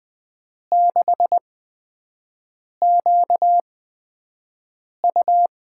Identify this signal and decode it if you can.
Morse code, heard as a keyed tone, about 20 wpm, spaced as for 5 wpm, 700 Hz. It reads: 6QU
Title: USA Call Signs - 8x Character Spacing - 20wpm